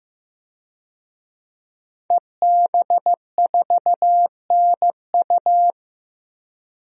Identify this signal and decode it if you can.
Morse code, heard as a keyed tone, about 16 wpm, 700 Hz. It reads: EB4NU
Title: DXCC-100 Call Signs - 5 Characters - Repeated 2x - 15wpm